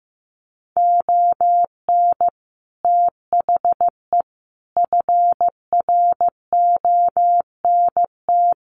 Morse code, heard as a keyed tone, about 15 wpm, 700 Hz. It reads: ON THE FRONT
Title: Sets of 5 Words Encore 15wpm